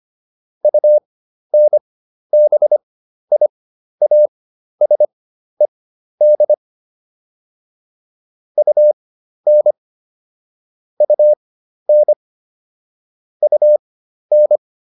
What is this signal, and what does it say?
Morse code, heard as a keyed tone, about 25 wpm, spaced as for 12 wpm, 600 Hz. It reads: UNBIASED UN UN UN